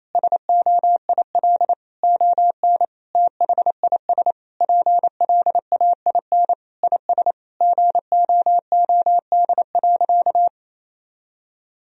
Morse code, heard as a keyed tone, about 28 wpm, 700 Hz. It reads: SOIL ON THIS PLAIN IS GOOD.
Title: Sentences from Top 1000 Words - Review Part 4 - 28wpm